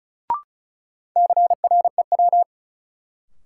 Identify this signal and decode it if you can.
Morse code, heard as a keyed tone, about 34 wpm, 700 Hz. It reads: CREW
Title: Sentences from Top 1500 Words - New Word Drill Alt - 35wpm